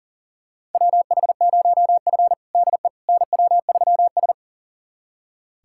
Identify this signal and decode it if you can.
Morse code, heard as a keyed tone, about 40 wpm, 700 Hz. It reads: WH0F DE NW3S
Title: USA Callsign DE Callsign - 4 Characters - 40wpm